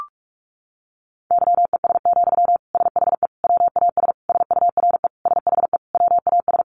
Morse code, heard as a keyed tone, about 46 wpm, 700 Hz. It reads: YES, SHE WAS SURE SHE WAS
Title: Sentences from Top 1100 Words 45wpm